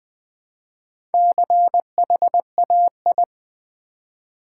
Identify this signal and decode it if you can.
Morse code, heard as a keyed tone, about 20 wpm, 700 Hz. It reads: CHAI
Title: Build A Word - 1 Letter At A Time - 5-Letter Words - 20wpm